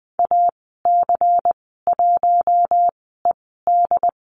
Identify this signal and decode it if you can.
Morse code, heard as a keyed tone, about 20 wpm, spaced as for 15 wpm, 700 Hz. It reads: AC1ED
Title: 179 Lesson 29g - Slash - Call Signs 20wpm